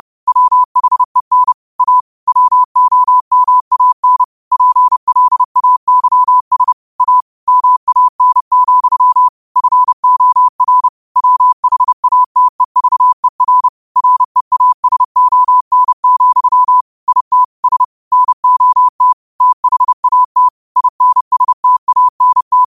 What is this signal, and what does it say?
Morse code, heard as a keyed tone, about 30 wpm, 1000 Hz. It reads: WHEN A WOMAN PLAYS A MAN, FOR WHATEVER REASON, IT S NOT THAT INSTANT
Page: New quotes every day in morse code at 30 Words per minute.